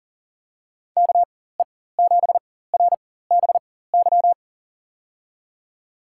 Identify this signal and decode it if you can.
Morse code, heard as a keyed tone, about 40 wpm, spaced as for 19 wpm, 700 Hz. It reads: KE7RBY
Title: USA Call Signs - 4x Character Spacing - 40wpm